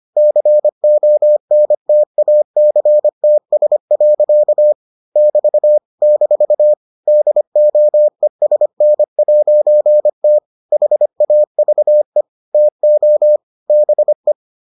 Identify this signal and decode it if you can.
Morse code, heard as a keyed tone, about 25 wpm, 600 Hz. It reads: CONTACTS. <BT> - DOESN'T HAVE TO BE